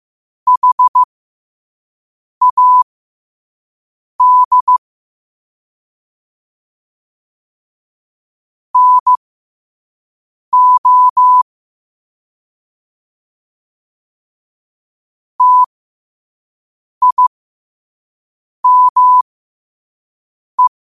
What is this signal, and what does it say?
Morse code, heard as a keyed tone, about 15 wpm, spaced as for 5 wpm, 1000 Hz. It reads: HAD NO TIME
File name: qotdfiles / 5wpm